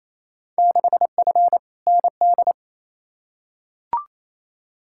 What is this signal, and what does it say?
Morse code, heard as a keyed tone, about 28 wpm, 700 Hz. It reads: 6F ND
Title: Field Day Exchanges Encore - 28wpm